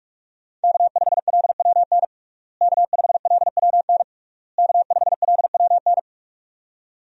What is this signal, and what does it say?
Morse code, heard as a keyed tone, about 46 wpm, 700 Hz. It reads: K5LWN K5LWN K5LWN
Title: USA Call Signs - 5 Characters - Repeated 3x - 45wpm